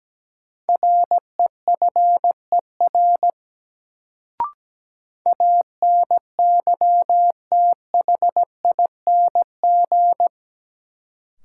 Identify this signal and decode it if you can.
Morse code, heard as a keyed tone, about 17 wpm, 700 Hz. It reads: REFER ANYTHING
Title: Sentences from Top 1000 Words - New Word Drill - 17wpm